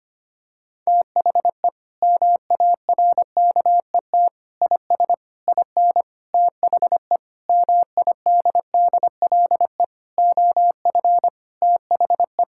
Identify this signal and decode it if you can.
Morse code, heard as a keyed tone, about 25 wpm, 700 Hz. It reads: THE MARKET IS IN THE MIDDLE OF THE
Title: Sentences from Top 1000 Words - Review Part 6 - 25wpm